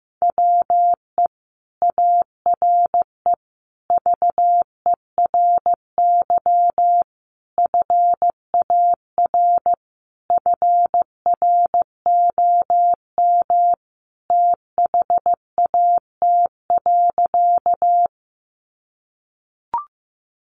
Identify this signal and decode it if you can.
Morse code, heard as a keyed tone, about 15 wpm, 700 Hz. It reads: WE ARE VERY FAR FROM THAT.
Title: Sentences from Top 300 Words 15wpm